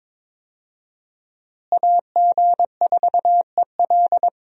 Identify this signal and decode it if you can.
Morse code, heard as a keyed tone, about 21 wpm, 700 Hz. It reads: AG4EL
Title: USA Call Signs - 5 Characters - Repeated 3x - 22wpm